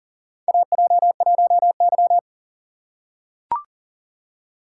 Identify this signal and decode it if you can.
Morse code, heard as a keyed tone, about 38 wpm, 700 Hz. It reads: AJ1Y
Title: USA Call Signs - 4 Characters - 6x Character Spacing - 40wpm